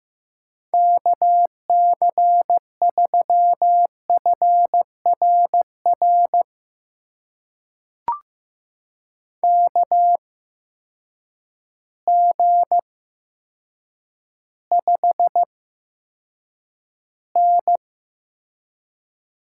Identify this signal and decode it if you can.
Morse code, heard as a keyed tone, about 15 wpm, spaced as for 8 wpm, 700 Hz. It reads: KC3FRR KG5N
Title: USA Call Signs - 6 Characters - 8x Character Spacing - 15wpm